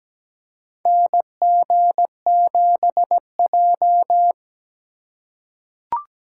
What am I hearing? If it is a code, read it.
Morse code, heard as a keyed tone, about 17 wpm, 700 Hz. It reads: NG7J
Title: USA Call Signs - 4 Characters - 3x Character Spacing - 17wpm